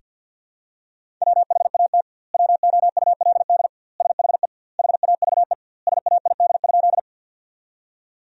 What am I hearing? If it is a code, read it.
Morse code, heard as a keyed tone, about 50 wpm, 700 Hz. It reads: WHAT WOULD SHE HAVE SAID?
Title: 2x Word-Spacing - Sentences from Top 100 Words - 50wpm